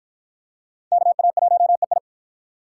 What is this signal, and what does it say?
Morse code, heard as a keyed tone, about 53 wpm, 700 Hz. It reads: KA1EI